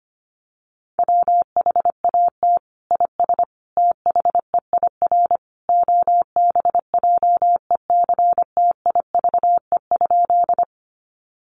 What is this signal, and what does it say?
Morse code, heard as a keyed tone, about 25 wpm, 700 Hz. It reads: WHAT IS THEIR OBJECTIVE?
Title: Sentences from Top 1200 Words 25wpm